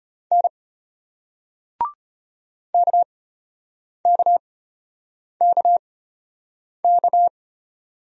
Morse code, heard as a keyed tone, about 34 wpm, spaced as for 8 wpm, 700 Hz. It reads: N KKKK